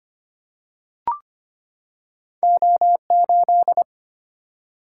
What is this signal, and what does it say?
Morse code, heard as a keyed tone, about 25 wpm, 700 Hz. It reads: O8
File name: Two Letter-Number Combinations - 25wpm